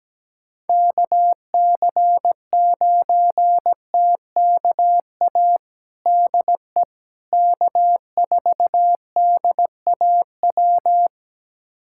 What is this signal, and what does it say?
Morse code, heard as a keyed tone, about 17 wpm, 700 Hz. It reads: KC9TKA DE K4DAW
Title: USA Callsign DE Callsign 17wpm